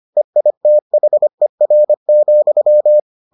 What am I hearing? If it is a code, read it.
Morse code, heard as a keyed tone, about 25 wpm, 600 Hz. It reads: EITHER,